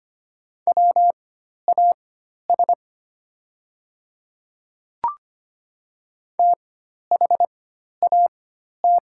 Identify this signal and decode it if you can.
Morse code, heard as a keyed tone, about 25 wpm, spaced as for 12 wpm, 700 Hz. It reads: WAS THAT
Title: 4x Character Spacing - Top 100 Words - 25wpm